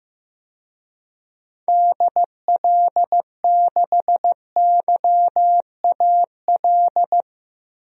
Morse code, heard as a keyed tone, about 16 wpm, 700 Hz. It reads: DL6YAL
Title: DXCC-100 Call Signs - 6 Characters - Repeated 3x - 15wpm